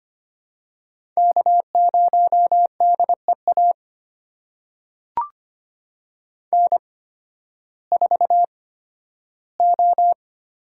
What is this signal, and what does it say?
Morse code, heard as a keyed tone, about 25 wpm, spaced as for 13 wpm, 700 Hz. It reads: K0DEA N4O
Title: USA Call Signs - 5 Characters - 8x Character Spacing - 25wpm